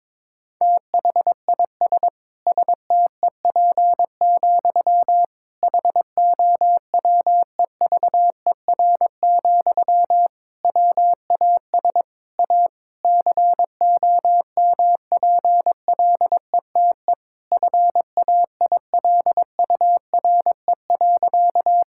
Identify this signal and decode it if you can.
Morse code, heard as a keyed tone, about 22 wpm, 700 Hz. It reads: THIS STEP, HOWEVER, WAS A COMPLETE FAILURE.
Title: Sentences from Top 1000 Words - Review Part 6 - 22wpm